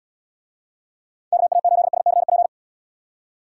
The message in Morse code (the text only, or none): KA8SCP